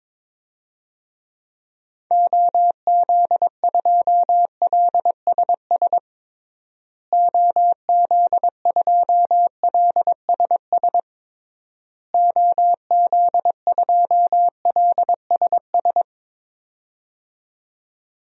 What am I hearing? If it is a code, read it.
Morse code, heard as a keyed tone, about 21 wpm, 700 Hz. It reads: OZ2LSS OZ2LSS OZ2LSS
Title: DXCC-100 Call Signs - 6 Characters - Repeated 3x - 22wpm